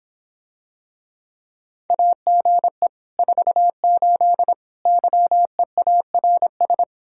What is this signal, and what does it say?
Morse code, heard as a keyed tone, about 26 wpm, 700 Hz. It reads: AGE 48 YEARS